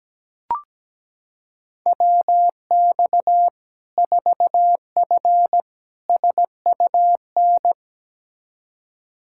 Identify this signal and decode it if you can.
Morse code, heard as a keyed tone, about 17 wpm, 700 Hz. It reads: WX 4F SUN
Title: Practice QSO - Exchange 2 - Info - 17wpm